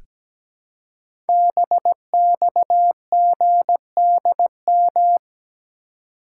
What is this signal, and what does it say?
Morse code, heard as a keyed tone, about 17 wpm, 700 Hz. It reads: BXGDM
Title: Code Groups - 5 Letters - 17wpm